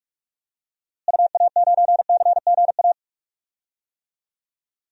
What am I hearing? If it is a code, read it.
Morse code, heard as a keyed tone, about 46 wpm, 700 Hz. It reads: UA9CGA
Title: DXCC-100 Call Signs - 6 Characters - Repeated 3x - 45wpm